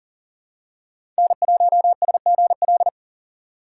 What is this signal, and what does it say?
Morse code, heard as a keyed tone, about 40 wpm, 700 Hz. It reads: N1SGL
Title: USA Call Signs - 2x Character Spacing - 40wpm